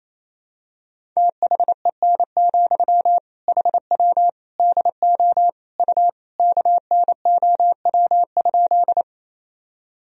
Morse code, heard as a keyed tone, about 28 wpm, 700 Hz. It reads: THEN, HW DO U KNOW?
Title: Sentences from Top 100 Words and Abbreviations 28wpm